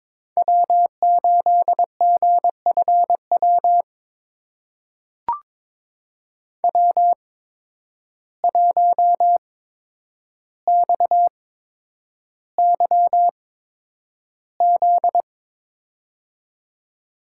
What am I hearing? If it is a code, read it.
Morse code, heard as a keyed tone, about 21 wpm, spaced as for 9 wpm, 700 Hz. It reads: W8GFW W1XYZ
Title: USA Call Signs - 5 Characters - 8x Character Spacing - 22wpm